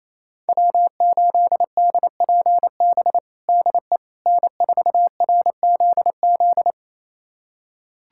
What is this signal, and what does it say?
Morse code, heard as a keyed tone, about 28 wpm, 700 Hz. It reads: W8DPB DE N4RZZ
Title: USA Callsign DE Callsign - 5 Characters - 28wpm